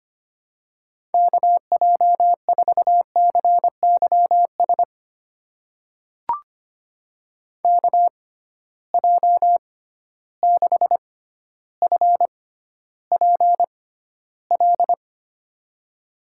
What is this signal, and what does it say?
Morse code, heard as a keyed tone, about 25 wpm, spaced as for 13 wpm, 700 Hz. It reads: KJ4CYS KJ6FPL
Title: USA Call Signs - 6 Characters - 6x Character Spacing - 25wpm